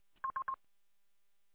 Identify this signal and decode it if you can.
There is no code.